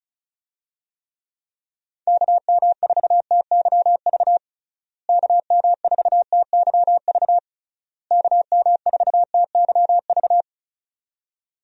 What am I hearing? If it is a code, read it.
Morse code, heard as a keyed tone, about 36 wpm, 700 Hz. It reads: KM4TYV KM4TYV KM4TYV
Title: USA Call Signs - 6 Characters - Repeated 3x - 35wpm